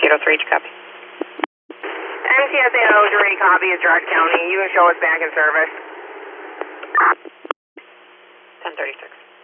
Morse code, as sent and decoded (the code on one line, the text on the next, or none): none